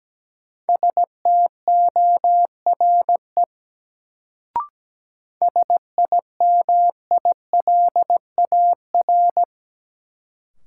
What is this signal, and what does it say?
Morse code, heard as a keyed tone, about 17 wpm, 700 Hz. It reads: STORE SIMILAR
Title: Sentences from Top 1000 Words - Review Part 5 - New Word Drill - 17wpm